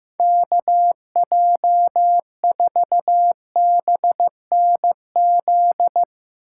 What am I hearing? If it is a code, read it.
Morse code, heard as a keyed tone, about 15 wpm, 700 Hz. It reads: KJ4BNZ
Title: Speed Racing - USA 2x3 Call Signs 15wpm